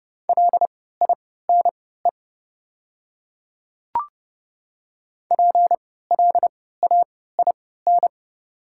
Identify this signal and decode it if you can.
Morse code, heard as a keyed tone, about 30 wpm, spaced as for 17 wpm, 690 Hz. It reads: LINE PLAIN
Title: 3x Character Spacing - Top 500 Words - 30wpm